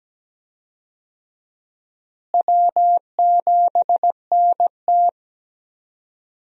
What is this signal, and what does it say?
Morse code, heard as a keyed tone, about 17 wpm, 700 Hz. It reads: W7NT
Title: USA Call Signs - 4 Characters - Repeated 3x - 17wpm